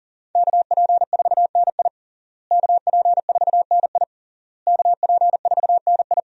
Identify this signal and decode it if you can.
Morse code, heard as a keyed tone, about 40 wpm, 700 Hz. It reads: KP4NI KP4NI KP4NI
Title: USA Call Signs - 5 Characters - Repeated 3x - 40wpm